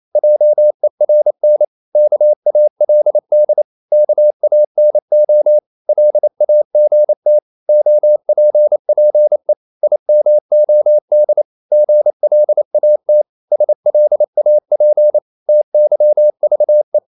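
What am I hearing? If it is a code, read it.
Morse code, heard as a keyed tone, about 28 wpm, 600 Hz. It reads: JERN KALD KANO LAGT OPPE IMOD GLAT SLAP TYVE
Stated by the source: Danish short words sent in Morse at 28 wpm, from CW med Gnister.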